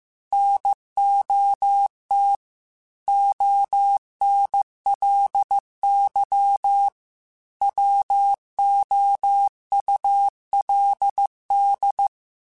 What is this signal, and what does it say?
Morse code, heard as a keyed tone, about 15 wpm, 800 Hz. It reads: NOT ONLY WOULD
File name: Technical Subjects / Technical Subjects-mp3s / AND Gate @13